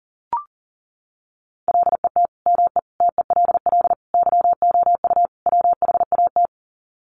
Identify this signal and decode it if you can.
Morse code, heard as a keyed tone, about 40 wpm, 700 Hz. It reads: LET ME TELL YOU WHAT
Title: Sets of 5 Words Encore 40wpm